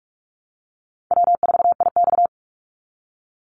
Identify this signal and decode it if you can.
Morse code, heard as a keyed tone, about 44 wpm, 700 Hz. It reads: W4IX
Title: Easy Call Signs Repeated 3x - 45wpm